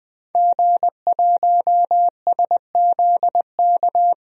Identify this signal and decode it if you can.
Morse code, heard as a keyed tone, about 20 wpm, 700 Hz. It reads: G1SZK
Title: DXCC-100 Call Signs - 5 Characters - 20wpm